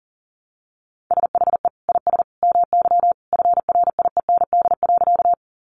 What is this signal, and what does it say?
Morse code, heard as a keyed tone, about 40 wpm, 700 Hz. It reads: SHE IS MY FRIEND.
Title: Sentences from Top 300 Words 40wpm